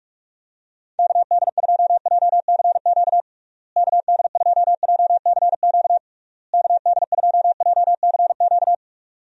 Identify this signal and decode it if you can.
Morse code, heard as a keyed tone, about 42 wpm, 700 Hz. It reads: KD2JCQ KD2JCQ KD2JCQ
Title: USA Call Signs - 6 Characters - Repeated 3x - 45wpm